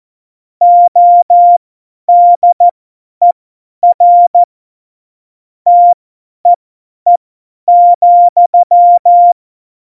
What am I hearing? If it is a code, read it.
Morse code, heard as a keyed tone, about 14 wpm, spaced as for 10 wpm, 700 Hz. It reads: ODER TEE,
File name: Morsekurs / Witze / Witz 18. Juni 2019, 50 F70